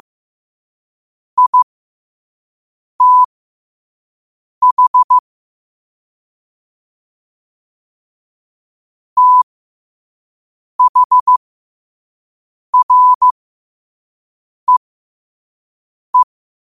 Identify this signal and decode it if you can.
Morse code, heard as a keyed tone, about 15 wpm, spaced as for 5 wpm, 1000 Hz. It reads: ITH THREE
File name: qotdfiles / 5wpm